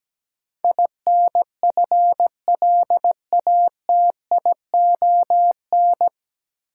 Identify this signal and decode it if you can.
Morse code, heard as a keyed tone, about 17 wpm, 700 Hz. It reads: INFLATION
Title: Sentences from Top 1100 Words - New Word Drill Alt - 17wpm